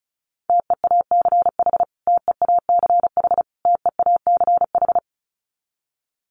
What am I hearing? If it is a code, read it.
Morse code, heard as a keyed tone, about 34 wpm, 700 Hz. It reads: TEACH TEACH TEACH
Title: Top 500 Words Repeated 35wpm